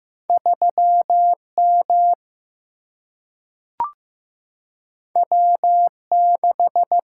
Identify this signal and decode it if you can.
Morse code, heard as a keyed tone, about 15 wpm, 700 Hz. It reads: 3 M W 6
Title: Single-Letter Single-Number 15wpm